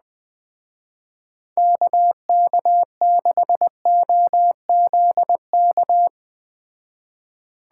Morse code, heard as a keyed tone, about 20 wpm, 700 Hz. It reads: KK6OZK
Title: Speed Racing - USA 2x3 Call Signs 17wpm